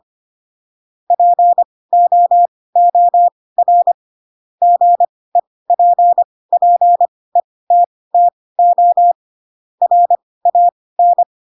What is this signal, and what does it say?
Morse code, heard as a keyed tone, about 25 wpm, spaced as for 18 wpm, 700 Hz. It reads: POOR GEPPETTO RAN
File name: Pinocchio ch3 12 25 ik1hge